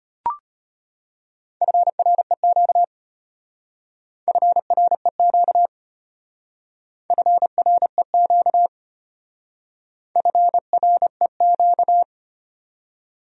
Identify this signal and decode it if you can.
Morse code, heard as a keyed tone, about 32 wpm, 700 Hz. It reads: FREQ FREQ FREQ FREQ